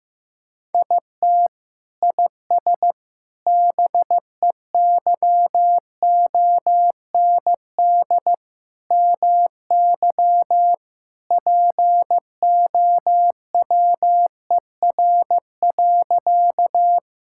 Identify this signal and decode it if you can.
Morse code, heard as a keyed tone, about 15 wpm, 700 Hz. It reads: IT IS BEYOND MY POWER.
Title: Sentences from Top 1000 Words - Review Part 6 - 15wpm